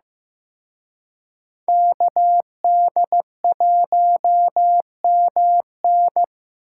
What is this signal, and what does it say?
Morse code, heard as a keyed tone, about 18 wpm, 700 Hz. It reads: KD1MN
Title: Speed Racing - USA Call Signs 15wpm